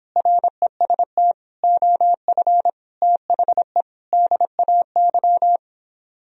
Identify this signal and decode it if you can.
Morse code, heard as a keyed tone, about 26 wpm, 700 Hz. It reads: REST OF THE DAY